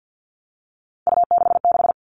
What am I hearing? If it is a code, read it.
Morse code, heard as a keyed tone, about 48 wpm, 700 Hz. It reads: U66
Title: All Cut Numbers - 3-Digit 50wpm